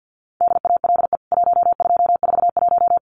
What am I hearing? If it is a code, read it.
Morse code, heard as a keyed tone, about 50 wpm, 700 Hz. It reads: DALE 1241
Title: CWT Exchange 50wpm